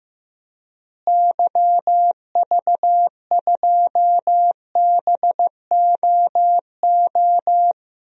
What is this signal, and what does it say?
Morse code, heard as a keyed tone, about 15 wpm, 700 Hz. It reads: YV2BOO
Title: DXCC-100 Call Signs - 6 Characters - Repeated 2x - 15wpm